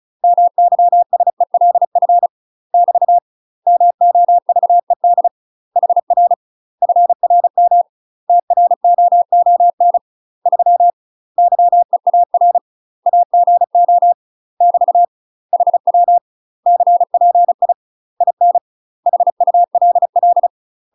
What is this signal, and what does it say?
Morse code, heard as a keyed tone, about 36 wpm, 700 Hz. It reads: MYSELF <BT> MOVED HR FRM TROON 3 YEAR AGO <BT> HW CPI IN HULL
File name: UK MASTER QSOs TRAINING / 35WPM / UK MASTER QSOs 0038